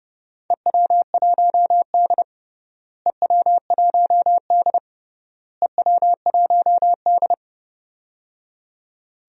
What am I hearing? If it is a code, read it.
Morse code, heard as a keyed tone, about 29 wpm, 700 Hz. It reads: EW1D EW1D EW1D
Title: DXCC-100 Call Signs - 4 Characters - Repeated 3x - 30wpm